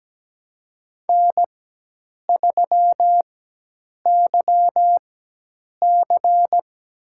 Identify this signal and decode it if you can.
Morse code, heard as a keyed tone, about 17 wpm, spaced as for 8 wpm, 700 Hz. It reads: N3YC
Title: USA Call Signs - 4x Character Spacing - 17wpm